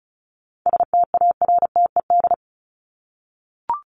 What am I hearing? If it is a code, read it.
Morse code, heard as a keyed tone, about 34 wpm, 700 Hz. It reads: STARTED
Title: ED Suffix - Top 100 Words with Base Words 35wpm